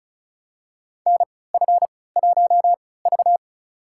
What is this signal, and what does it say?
Morse code, heard as a keyed tone, about 36 wpm, spaced as for 20 wpm, 700 Hz. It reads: NF1V
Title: USA Call Signs - 4 Characters - 3x Character Spacing - 35wpm